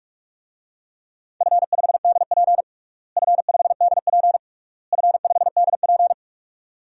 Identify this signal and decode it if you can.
Morse code, heard as a keyed tone, about 46 wpm, 700 Hz. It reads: F5DP F5DP F5DP
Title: DXCC-100 Call Signs - 4 Characters - Repeated 3x - 45wpm